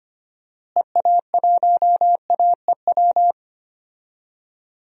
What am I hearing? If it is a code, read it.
Morse code, heard as a keyed tone, about 25 wpm, 700 Hz. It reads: EA1AEW
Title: DXCC-100 Call Signs - 6 Characters - Repeated 3x - 25wpm